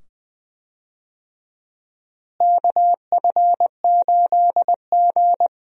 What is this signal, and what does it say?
Morse code, heard as a keyed tone, about 20 wpm, 700 Hz. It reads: KF8G
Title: USA Call Signs - 4 Characters - Repeated 2x - 20wpm